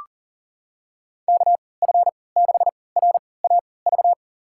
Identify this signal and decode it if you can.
Morse code, heard as a keyed tone, about 40 wpm, spaced as for 23 wpm, 700 Hz. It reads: KF6RAV
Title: USA Call Signs - 3x Character Spacing - 40wpm